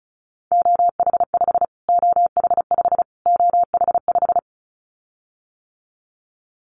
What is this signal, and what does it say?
Morse code, heard as a keyed tone, about 34 wpm, 700 Hz. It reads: OH5 OH5 OH5
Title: Callsign Prefixes for Top 25 DXCC - Repeated 3x - 35wpm